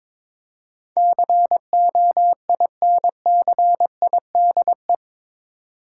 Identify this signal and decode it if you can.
Morse code, heard as a keyed tone, about 22 wpm, 700 Hz. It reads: COINCIDE
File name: practice / 22wpm / Eight Letter Words - 22wpm